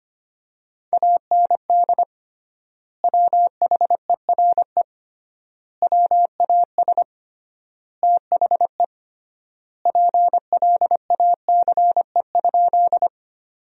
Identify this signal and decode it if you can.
Morse code, heard as a keyed tone, about 25 wpm, 700 Hz. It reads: AND WHERE WAS THE PLACE?
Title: 3x Word-Spacing - Sentences from Top 200 Words - 25wpm